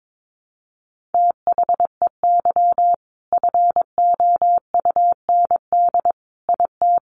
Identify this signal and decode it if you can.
Morse code, heard as a keyed tone, about 22 wpm, 700 Hz. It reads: THEY FOUND IT
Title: Sentences from Top 1400 Words 22wpm